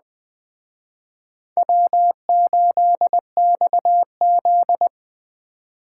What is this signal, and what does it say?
Morse code, heard as a keyed tone, about 20 wpm, 700 Hz. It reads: W8XZ
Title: Speed Racing - USA 1x2 Call Signs 17wpm